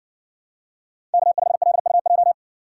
Morse code, heard as a keyed tone, about 58 wpm, 700 Hz. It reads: K5LUJ